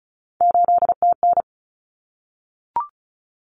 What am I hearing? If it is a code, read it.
Morse code, heard as a keyed tone, about 34 wpm, 700 Hz. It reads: 8TN
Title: Common Cut Numbers - 3-Digit 35wpm